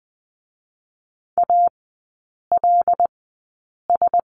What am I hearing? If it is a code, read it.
Morse code, heard as a keyed tone, about 20 wpm, spaced as for 8 wpm, 700 Hz. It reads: ALS
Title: 074 Lesson 13e - Letter L - Three Characters 20wpm